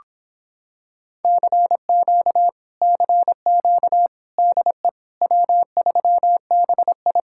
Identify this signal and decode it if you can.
Morse code, heard as a keyed tone, about 26 wpm, 700 Hz. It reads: CQ CQ DE W3BI